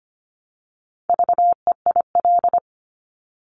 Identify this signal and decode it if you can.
Morse code, heard as a keyed tone, about 25 wpm, 700 Hz. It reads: VEIL
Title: Multi-letter Phonemes Repeated 3x 25wpm